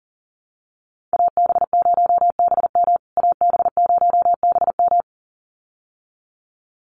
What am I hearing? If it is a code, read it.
Morse code, heard as a keyed tone, about 37 wpm, 700 Hz. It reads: AB0BM AB0BM
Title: USA Call Signs Repeated 2x 40wpm